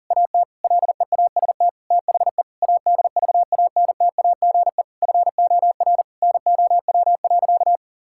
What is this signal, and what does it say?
Morse code, heard as a keyed tone, about 40 wpm, 700 Hz. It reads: AT LEAST THE ADVANTAGE FOR NOW.
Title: Sentences from Top 1000 Words - Review Part 6 - 40wpm